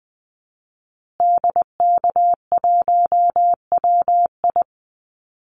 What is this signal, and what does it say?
Morse code, heard as a keyed tone, about 20 wpm, 700 Hz. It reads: DK1WI
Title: Call Signs Repeated 2x - 20wpm